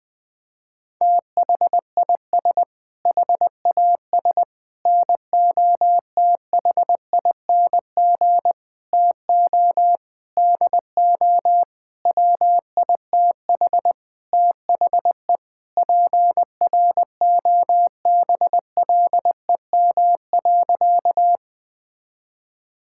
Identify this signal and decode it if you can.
Morse code, heard as a keyed tone, about 20 wpm, 700 Hz. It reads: THIS HAS NOTHING TO DO WITH THE PROBLEM.
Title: Sentences with 5-Word Phrases - Spoken Beginning - 20wpm